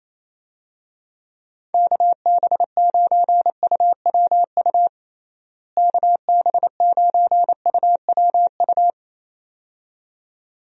Morse code, heard as a keyed tone, about 28 wpm, 700 Hz. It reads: KB9UWU KB9UWU
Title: USA Call Signs - 6 Characters - Repeated 2x - 28wpm